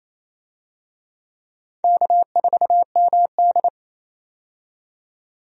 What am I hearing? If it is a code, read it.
Morse code, heard as a keyed tone, about 28 wpm, 700 Hz. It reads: K4MD
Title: USA Call Signs - 4 Characters - 28wpm